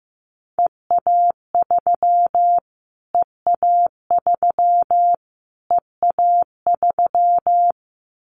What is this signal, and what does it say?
Morse code, heard as a keyed tone, about 15 wpm, 700 Hz. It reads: EA3 EA3 EA3
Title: Callsign Prefixes for Top 25 DXCC - Repeated 3x - 15wpm